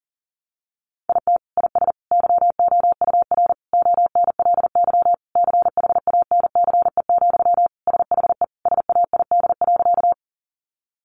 Morse code, heard as a keyed tone, about 40 wpm, 700 Hz. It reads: IT IS YOUR ONLY CHANCE, SHE SAID.
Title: Sentences from Top 800 Words 40wpm